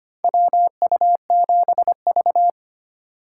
Morse code, heard as a keyed tone, about 25 wpm, 700 Hz. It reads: WU7V
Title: USA Call Signs - 4 Characters - 3x Character Spacing - 25wpm